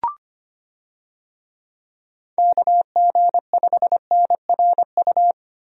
Morse code, heard as a keyed tone, about 25 wpm, 700 Hz. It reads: KG5NRU
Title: USA Call Signs - 6 Characters - Repeated 3x - 25wpm